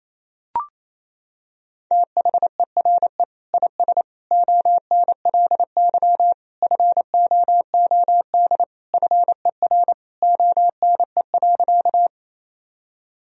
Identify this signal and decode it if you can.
Morse code, heard as a keyed tone, about 28 wpm, 700 Hz. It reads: THERE IS ONLY FOOD FER ONE.
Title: Sentences from Top 300 Words and Abbreviations 28wpm